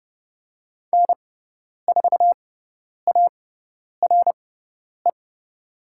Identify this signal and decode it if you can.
Morse code, heard as a keyed tone, about 30 wpm, spaced as for 10 wpm, 700 Hz. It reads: N4ARE
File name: Lesson 16g - Letter c - Call Signs 30wpm